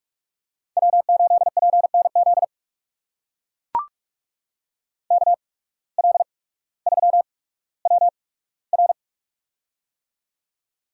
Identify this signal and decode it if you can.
Morse code, heard as a keyed tone, about 44 wpm, spaced as for 19 wpm, 700 Hz. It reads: W8PNZ KL3WR
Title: USA Call Signs - 5 Characters - 8x Character Spacing - 45wpm